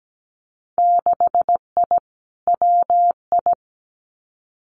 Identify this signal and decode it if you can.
Morse code, heard as a keyed tone, about 17 wpm, 700 Hz. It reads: 6I WI
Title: Winter Field Day Exchanges Encore 17wpm